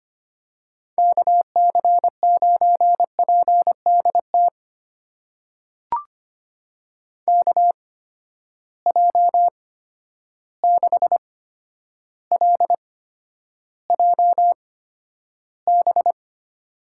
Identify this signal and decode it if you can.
Morse code, heard as a keyed tone, about 25 wpm, spaced as for 11 wpm, 700 Hz. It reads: KC9PDT KJ6LJB
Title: USA Call Signs - 6 Characters - 8x Character Spacing - 25wpm